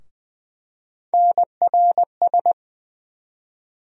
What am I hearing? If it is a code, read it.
Morse code, heard as a keyed tone, about 20 wpm, 700 Hz. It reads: NRS
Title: USA Call Sign Postfixes - 3 Letters 20wpm